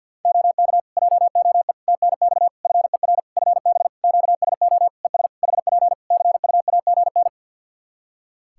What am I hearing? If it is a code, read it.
Morse code, heard as a keyed tone, about 46 wpm, 700 Hz. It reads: OK JOE TNX FER FB QSO ES HP CUAGN
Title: K1USN SST QSO Practice 50wpm